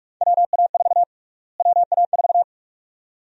Morse code, heard as a keyed tone, about 47 wpm, 700 Hz. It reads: WA4 WA4
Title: USA Call Sign Prefixes - Repeated 2x - 45wpm